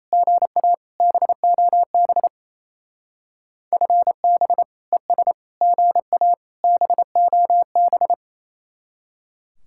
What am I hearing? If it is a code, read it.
Morse code, heard as a keyed tone, about 34 wpm, 700 Hz. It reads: GA BOB FB ES GA BOB
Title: Speed-Racing Common QSO Phrases - 28wpm